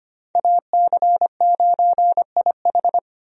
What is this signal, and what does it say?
Morse code, heard as a keyed tone, about 25 wpm, 700 Hz. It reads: AC9IH
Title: USA Call Signs - 5 Characters - 25wpm